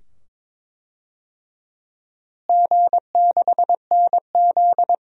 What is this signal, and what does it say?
Morse code, heard as a keyed tone, about 22 wpm, 700 Hz. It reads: G6NZ
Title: DXCC-100 Call Signs - 4 Characters - Repeated 2x - 22wpm